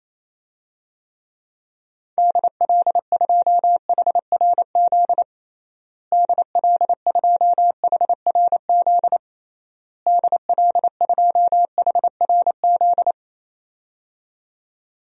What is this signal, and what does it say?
Morse code, heard as a keyed tone, about 28 wpm, 700 Hz. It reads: DL2HRZ DL2HRZ DL2HRZ
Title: DXCC-100 Call Signs - 6 Characters - Repeated 3x - 28wpm